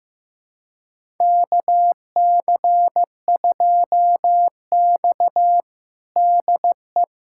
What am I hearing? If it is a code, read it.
Morse code, heard as a keyed tone, about 15 wpm, 700 Hz. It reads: KC2X DE
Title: USA Callsign DE Callsign - 4 Characters - 15wpm